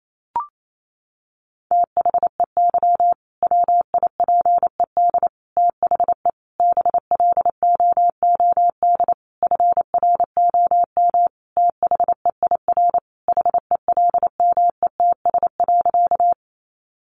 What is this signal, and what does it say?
Morse code, heard as a keyed tone, about 28 wpm, 700 Hz. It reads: THEY WIPED THE BLOOD FROM THEIR HELMETS.
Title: A Sound Of Thunder 28wpm